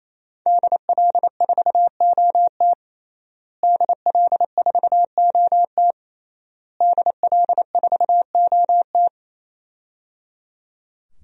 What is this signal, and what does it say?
Morse code, heard as a keyed tone, about 27 wpm, 700 Hz. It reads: DL4OT DL4OT DL4OT
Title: USA Call Signs - 5 Characters - Repeated 3x - 28wpm